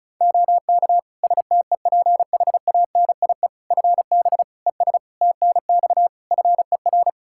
Morse code, heard as a keyed tone, about 36 wpm, 700 Hz. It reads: OK STEPHANIE FB ES TNX FER
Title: Practice QSO - Exchange 2 - Info - 35wpm